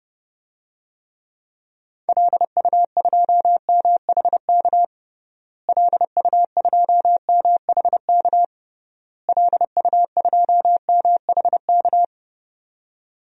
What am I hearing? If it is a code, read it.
Morse code, heard as a keyed tone, about 32 wpm, 700 Hz. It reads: LU2MHK LU2MHK LU2MHK
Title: DXCC-100 Call Signs - 6 Characters - Repeated 3x - 30wpm